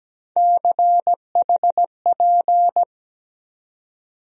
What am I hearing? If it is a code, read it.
Morse code, heard as a keyed tone, about 17 wpm, 700 Hz. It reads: CHP
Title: USA Call Sign Postfixes - 3 Letters 17wpm